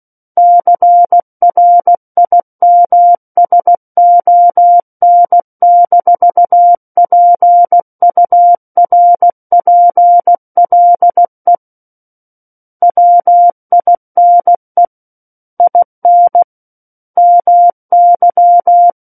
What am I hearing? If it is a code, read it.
Morse code, heard as a keyed tone, about 16 wpm, 700 Hz. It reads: CRIMSON-PURPLE WINE IN MY
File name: Book One / 07-Chapter 7 - 16 WPM